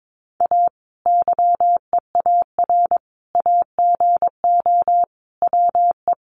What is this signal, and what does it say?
Morse code, heard as a keyed tone, about 22 wpm, 700 Hz. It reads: A YEAR AGO WE
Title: Sentences from Top 1400 Words 22wpm